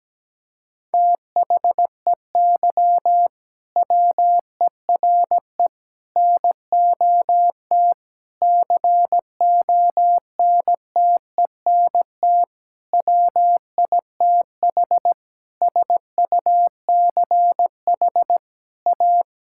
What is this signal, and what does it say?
Morse code, heard as a keyed tone, about 17 wpm, 700 Hz. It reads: THEY WERE NOT CONTENT WITH SUCH A
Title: Sentences from Top 1000 Words - Review Part 6 - 17wpm